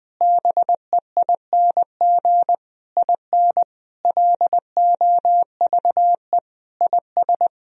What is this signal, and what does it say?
Morse code, heard as a keyed tone, about 21 wpm, 700 Hz. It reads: BEING IN LOVE IS
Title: Sentences from Top 1000 Words - Review Part 3 - 20wpm